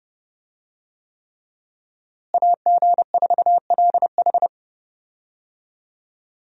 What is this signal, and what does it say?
Morse code, heard as a keyed tone, about 30 wpm, 700 Hz. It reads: AG4LH
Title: USA Call Signs - 5 Characters - Repeated 3x - 30wpm